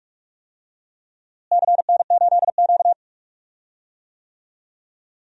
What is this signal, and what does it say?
Morse code, heard as a keyed tone, about 42 wpm, 700 Hz. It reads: CN8Q
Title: DXCC-100 Call Signs - 4 Characters - Repeated 2x - 45wpm